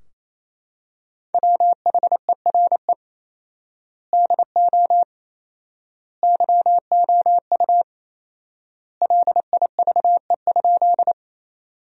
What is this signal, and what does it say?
Morse code, heard as a keyed tone, about 28 wpm, 700 Hz. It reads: WHERE DO YOU LIVE?
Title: 4x Word-Spacing - Sentences from Top 200 Words - 28wpm